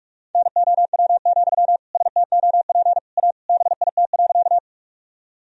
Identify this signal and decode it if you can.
Morse code, heard as a keyed tone, about 41 wpm, 700 Hz. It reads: NOW, STOP A BIT.
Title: Sentences from Top 1000 Words - Review Part 1 - 45wpm